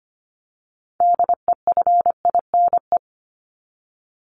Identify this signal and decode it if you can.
Morse code, heard as a keyed tone, about 25 wpm, 700 Hz. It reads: DEFINE
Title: ED Suffix - Top 100 Words with Base Words 25wpm